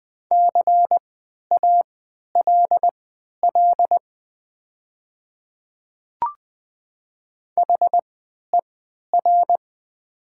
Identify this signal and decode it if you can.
Morse code, heard as a keyed tone, about 20 wpm, spaced as for 11 wpm, 700 Hz. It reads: CALL HER
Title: 3x Character Spacing - Top 100 Words - 20wpm